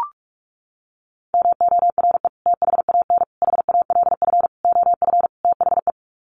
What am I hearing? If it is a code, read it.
Morse code, heard as a keyed tone, about 44 wpm, 700 Hz. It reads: MORE THAN HALF OF THE
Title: Sets of 5 Words 45wpm